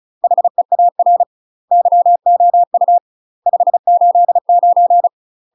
Morse code, heard as a keyed tone, about 36 wpm, 700 Hz. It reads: HEAR YOU 589
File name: UK MASTER QSOs TRAINING / 35WPM / UK MASTER QSOs 0020